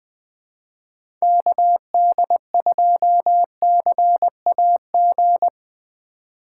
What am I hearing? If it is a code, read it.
Morse code, heard as a keyed tone, about 20 wpm, 700 Hz. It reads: KD2CAG
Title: USA Call Signs - 6 Characters - 6x Character Spacing - 20wpm